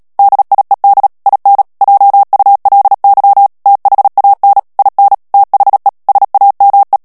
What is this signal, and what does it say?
Morse code, heard as a keyed tone, about 37 wpm, 800 Hz. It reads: DIED IN JULY THAN IN THE SAME
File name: morse mp3 / ANGLAIS texte clair 2 vitesse 35 mots minute